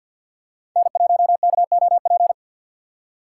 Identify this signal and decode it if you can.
Morse code, heard as a keyed tone, about 48 wpm, 700 Hz. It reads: N1KOP
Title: USA Call Signs - 6x Character Spacing - 50wpm